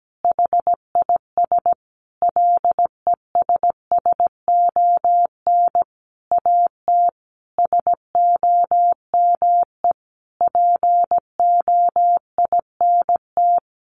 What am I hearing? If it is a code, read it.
Morse code, heard as a keyed tone, about 17 wpm, 700 Hz. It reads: HIS LESSON AT SOME POINT
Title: Sentences from Top 1200 Words 17wpm